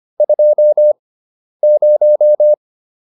Morse code, heard as a keyed tone, about 25 wpm, spaced as for 10 wpm, 600 Hz. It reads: 20